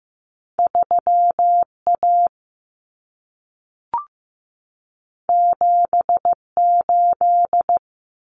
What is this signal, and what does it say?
Morse code, heard as a keyed tone, about 15 wpm, 700 Hz. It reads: 3A 78
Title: Common Cut Numbers - 2-Digit 15wpm